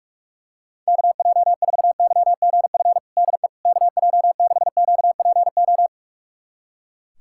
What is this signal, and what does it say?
Morse code, heard as a keyed tone, about 41 wpm, 700 Hz. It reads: KJ4YGF DE KJ6QPO
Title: USA Callsign DE Callsign 45wpm